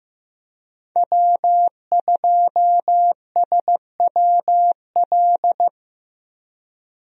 Morse code, heard as a keyed tone, about 15 wpm, 700 Hz. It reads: W2SWL
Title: USA Call Signs - 5 Characters - Repeated 2x - 15wpm